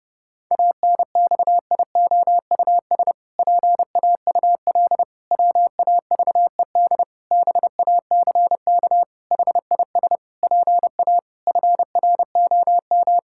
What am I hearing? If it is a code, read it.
Morse code, heard as a keyed tone, about 30 wpm, 700 Hz. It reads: ANXIOUS PAUL WAVED BACK HIS PA FROM
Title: Pangram Challenge v2 - 30wpm